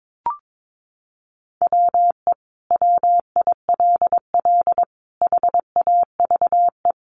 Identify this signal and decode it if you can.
Morse code, heard as a keyed tone, about 22 wpm, 700 Hz. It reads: WE WILL HAVE
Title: Sentences from Top 1200 Words 22wpm